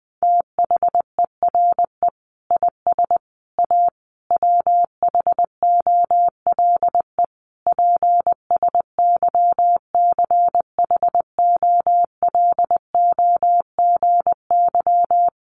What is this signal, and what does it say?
Morse code, heard as a keyed tone, about 20 wpm, 700 Hz. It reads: THERE IS A WHOLE PSYCHOLOGY
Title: Sentences from Top 1400 Words 20wpm